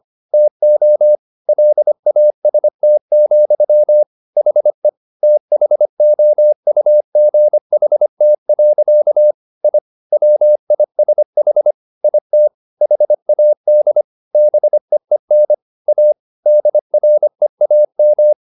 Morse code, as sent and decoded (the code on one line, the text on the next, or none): - --- / .-.. .- ... - --..-- / .... / - .... --- ..- --. .... - .-.-.- / .. / .-- .. ... .... / .. - / .... .- -.. / -... . -. / .- / -.. .-. .- --
TO LAST, HE THOUGHT. I WISH IT HAD BEEN A DREAM